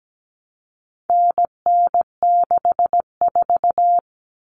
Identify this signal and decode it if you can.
Morse code, heard as a keyed tone, about 17 wpm, 700 Hz. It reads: NN64
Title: Common Cut Numbers - 4-Digit 17wpm